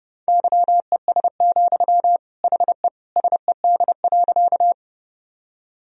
Morse code, heard as a keyed tone, about 28 wpm, 700 Hz. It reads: YES, HE SED.
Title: Sentences from Top 500 Words and Abbreviations 30wpm